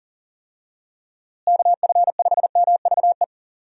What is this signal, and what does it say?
Morse code, heard as a keyed tone, about 40 wpm, 700 Hz. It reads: KF5MVE
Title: USA Call Signs - 6 Characters - Repeated 3x - 40wpm